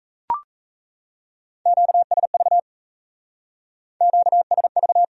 Morse code, heard as a keyed tone, about 34 wpm, 700 Hz. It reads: QSV QSV
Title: Speed-Racing Q-Codes - 28wpm